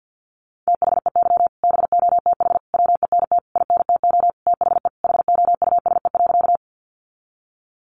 Morse code, heard as a keyed tone, about 50 wpm, 700 Hz. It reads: THEY BOTH WENT INTO THE HOUSE.
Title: Sentences from Top 600 Words 50wpm